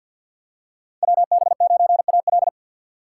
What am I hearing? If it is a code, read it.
Morse code, heard as a keyed tone, about 52 wpm, 700 Hz. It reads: WB9AL
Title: USA Call Signs - 5 Characters - 4x Character Spacing - 50wpm